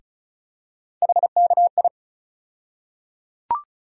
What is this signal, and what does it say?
Morse code, heard as a keyed tone, about 34 wpm, 700 Hz. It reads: HKI
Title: USA Call Sign Postfixes - 3 Letters 35wpm